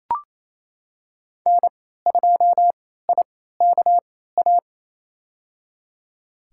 Morse code, heard as a keyed tone, about 28 wpm, spaced as for 16 wpm, 700 Hz. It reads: N2IKA
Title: USA Call Signs - 3x Character Spacing - 28wpm